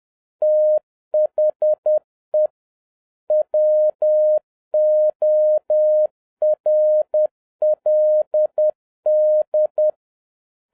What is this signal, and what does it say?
Morse code, heard as a keyed tone, about 10 wpm, 610 Hz. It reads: THE WORLD